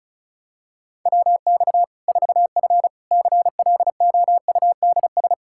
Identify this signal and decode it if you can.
Morse code, heard as a keyed tone, about 36 wpm, 700 Hz. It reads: WX 4F CLOUDS
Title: Practice QSO - Exchange 2 - Info - 35wpm